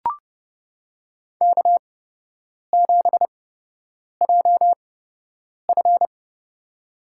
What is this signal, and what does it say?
Morse code, heard as a keyed tone, about 30 wpm, spaced as for 8 wpm, 700 Hz. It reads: K7JF
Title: USA Call Signs - 4 Characters - 8x Character Spacing - 30wpm